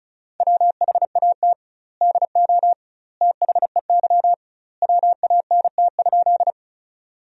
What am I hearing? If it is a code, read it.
Morse code, heard as a keyed tone, about 34 wpm, 700 Hz. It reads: WHAT DO THEY WANT?
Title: 2x Word-Spacing - Sentences from Top 200 Words - 35wpm